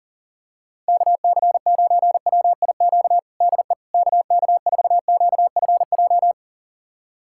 Morse code, heard as a keyed tone, about 40 wpm, 700 Hz. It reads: KC9WIQ DE KK4QFJ
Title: USA Callsign DE Callsign - 6 Characters - 40wpm